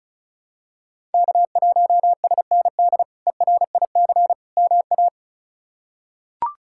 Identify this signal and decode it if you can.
Morse code, heard as a keyed tone, about 36 wpm, 700 Hz. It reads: K1SND ERIC MA
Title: K1USN SST QSO Practice - 35wpm